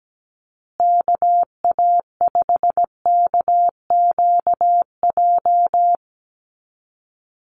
Morse code, heard as a keyed tone, about 17 wpm, 700 Hz. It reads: KA5KQJ
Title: USA Call Signs Repeated 2x 17wpm